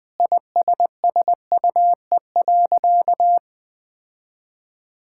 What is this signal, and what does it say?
Morse code, heard as a keyed tone, about 20 wpm, 700 Hz. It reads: ISSUE.
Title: Sentences with 7-Word Phrases - 20wpm